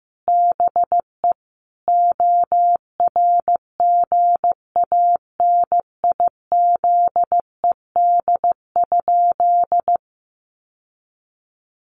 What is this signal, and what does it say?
Morse code, heard as a keyed tone, about 15 wpm, 700 Hz. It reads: BE ORGANIZED?
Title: Sentences from Top 1300 Words 15wpm